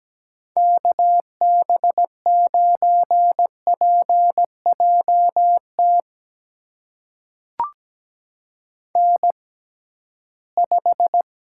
Morse code, heard as a keyed tone, about 17 wpm, spaced as for 13 wpm, 700 Hz. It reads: KB9PJT N5
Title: USA Call Signs - 6x Character Spacing - 17wpm